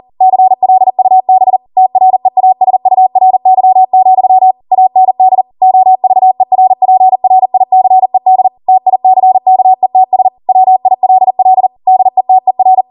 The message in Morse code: CLUB TREASURY, AND OVERPRICED TICKETS WILL DETER